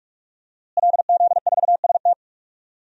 Morse code, heard as a keyed tone, about 46 wpm, 700 Hz. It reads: LZ4ST
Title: DXCC-100 Call Signs - 5 Characters - Repeated 2x - 45wpm